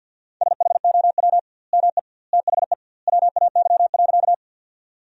Morse code, heard as a keyed tone, about 50 wpm, 700 Hz. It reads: SHOW ME THE WAY.
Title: 2x Word-Spacing - Sentences from Top 200 Words - 50wpm